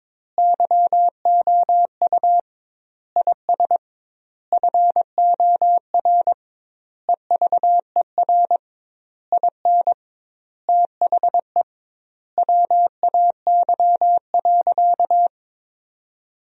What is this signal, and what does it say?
Morse code, heard as a keyed tone, about 22 wpm, 700 Hz. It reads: YOU IS FOR EVER IN THE WAY.
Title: 2x Word-Spacing - Sentences from Top 400 Words - 22wpm